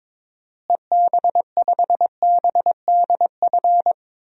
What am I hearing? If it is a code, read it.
Morse code, heard as a keyed tone, about 22 wpm, 700 Hz. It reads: EB5BDF
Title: DXCC-100 Call Signs - 6 Characters - Repeated 3x - 22wpm